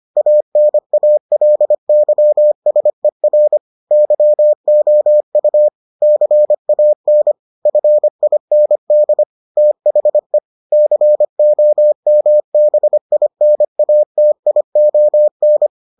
Morse code, heard as a keyed tone, about 25 wpm, 600 Hz. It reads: ANALYSER YOU CAN FIND THE COMBINATION